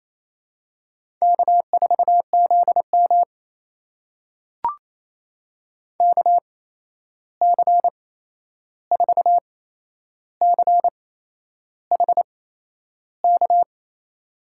Morse code, heard as a keyed tone, about 28 wpm, spaced as for 11 wpm, 700 Hz. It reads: K4ZM KC4CHK
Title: USA Call Signs - 8x Character Spacing - 28wpm